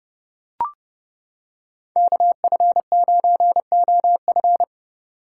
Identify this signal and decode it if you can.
Morse code, heard as a keyed tone, about 29 wpm, 700 Hz. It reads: KF9OF